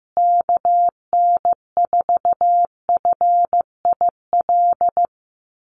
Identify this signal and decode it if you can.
Morse code, heard as a keyed tone, about 15 wpm, 700 Hz. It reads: KN4FIL
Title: USA Call Signs Repeated 3x 15wpm